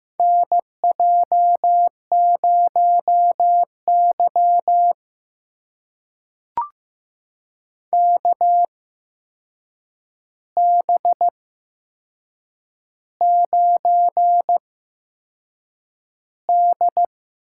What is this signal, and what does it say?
Morse code, heard as a keyed tone, about 15 wpm, spaced as for 6 wpm, 700 Hz. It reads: NJ0Y KB9D
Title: USA Call Signs - 4 Characters - 8x Character Spacing - 15wpm